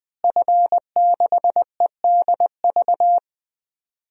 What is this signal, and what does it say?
Morse code, heard as a keyed tone, about 20 wpm, 700 Hz. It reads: F6EDV
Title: DXCC-100 Call Signs - 5 Characters - Repeated 2x - 20wpm